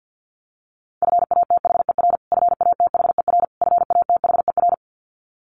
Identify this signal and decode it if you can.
Morse code, heard as a keyed tone, about 50 wpm, 700 Hz. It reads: FATHER FATHER FATHER
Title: Top 500 Words Repeated 50wpm